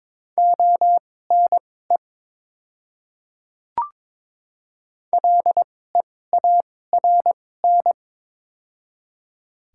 Morse code, heard as a keyed tone, about 22 wpm, spaced as for 16 wpm, 700 Hz. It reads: ONE LEARN
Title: 2x Character Spacing - Top 300 Words - 22wpm